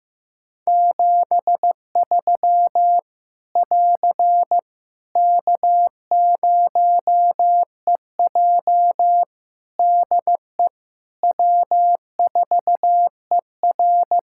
Morse code, heard as a keyed tone, about 15 wpm, 700 Hz. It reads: 73 <AR> K0EJ DE W4ER
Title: Practice QSO - Exchange 4 - Ending - 15wpm